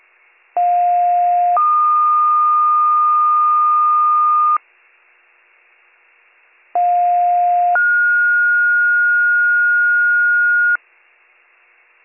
Two-tone page, heard as a one-second tone, then a three-second tone, about 710 then 1220 Hz, about 710 then 1430 Hz. It is heard from 0.6 to 10.8 s.